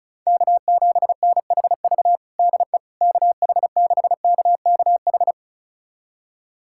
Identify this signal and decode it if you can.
Morse code, heard as a keyed tone, about 36 wpm, 700 Hz. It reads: K7NHV DE KH6KKH
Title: USA Callsign DE Callsign 35wpm